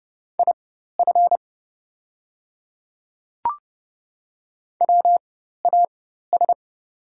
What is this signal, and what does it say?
Morse code, heard as a keyed tone, about 30 wpm, spaced as for 14 wpm, 700 Hz. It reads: IF WAS
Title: 4x Character Spacing - Top 100 Words - 30wpm